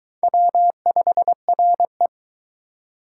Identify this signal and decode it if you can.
Morse code, heard as a keyed tone, about 22 wpm, 700 Hz. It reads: W5RE